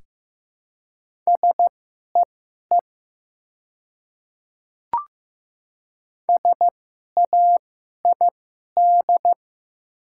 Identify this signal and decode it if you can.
Morse code, heard as a keyed tone, about 15 wpm, spaced as for 11 wpm, 700 Hz. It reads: SEE SAID
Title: 2x Character Spacing - Top 100 Words - 15wpm